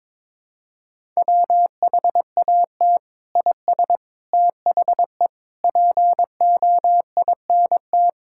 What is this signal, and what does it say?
Morse code, heard as a keyed tone, about 22 wpm, 700 Hz. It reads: WHAT IS THE POINT
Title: Sentences from Top 1000 Words - Review Part 5 - 22wpm